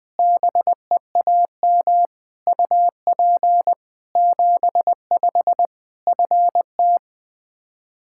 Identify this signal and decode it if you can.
Morse code, heard as a keyed tone, about 20 wpm, 700 Hz. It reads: BEAM UP 75 FT
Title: Practice QSO - Exchange 2 - Info - 20wpm